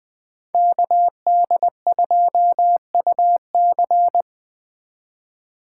Morse code, heard as a keyed tone, about 20 wpm, 700 Hz. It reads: KD2UC